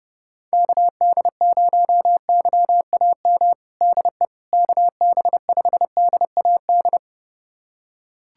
Morse code, heard as a keyed tone, about 30 wpm, 700 Hz. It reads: KD0YAM DE KB5DAD
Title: USA Callsign DE Callsign - 6 Characters - 30wpm